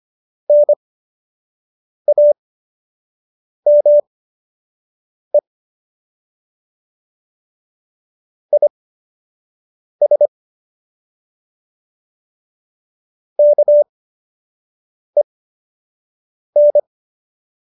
Morse code, heard as a keyed tone, about 25 wpm, spaced as for 6 wpm, 590 Hz. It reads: NAME IS KEN
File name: qso003 06